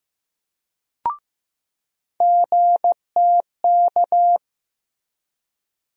Morse code, heard as a keyed tone, about 15 wpm, 700 Hz. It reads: GTK
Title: USA Call Sign Postfixes - 3 Letters 15wpm